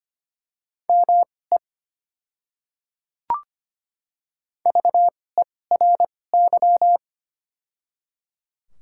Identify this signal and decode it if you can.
Morse code, heard as a keyed tone, about 25 wpm, spaced as for 18 wpm, 700 Hz. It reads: ME VERY
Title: 2x Character Spacing - Top 500 Words - 25wpm